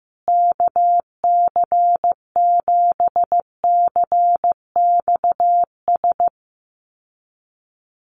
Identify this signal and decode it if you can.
Morse code, heard as a keyed tone, about 15 wpm, 700 Hz. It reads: KC7CXS
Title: USA Call Signs Repeated 3x 15wpm